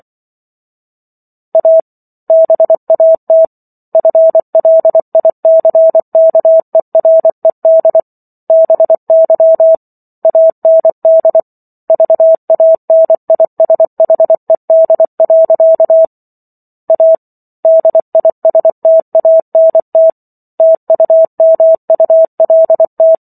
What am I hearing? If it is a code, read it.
Morse code, heard as a keyed tone, about 24 wpm, 650 Hz. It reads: A BAT FLICKERED BY AND VANISHED. A DISTANT TUMULT